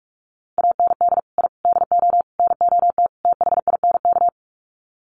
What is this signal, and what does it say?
Morse code, heard as a keyed tone, about 44 wpm, 700 Hz. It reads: AND I DO NOT THINK
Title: Sets of 5 Words 45wpm